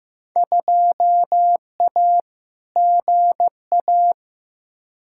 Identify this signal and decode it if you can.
Morse code, heard as a keyed tone, about 15 wpm, 700 Hz. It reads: 2A GA
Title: Field Day Exchanges - 15wpm